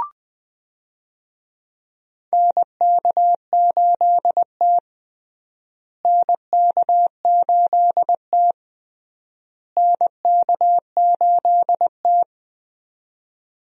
Morse code, heard as a keyed tone, about 20 wpm, 700 Hz. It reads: NK8T NK8T NK8T
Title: USA Call Signs - 4 Characters - Repeated 3x - 20wpm